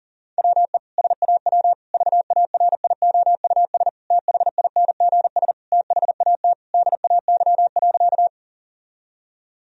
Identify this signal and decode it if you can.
Morse code, heard as a keyed tone, about 40 wpm, 700 Hz. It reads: WE SAW VARIOUS THINGS THAT DAY.
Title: Sentences from Top 1000 Words - Review Part 4 - 40wpm